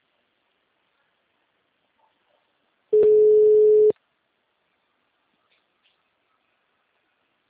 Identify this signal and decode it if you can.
Ringback tone (Europe, 425 Hz).
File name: out- -2051- - - .53524